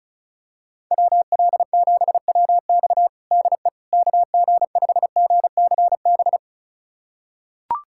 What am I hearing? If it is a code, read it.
Morse code, heard as a keyed tone, about 36 wpm, 700 Hz. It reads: WL7WX DE KG5GCB
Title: USA Callsign DE Callsign 35wpm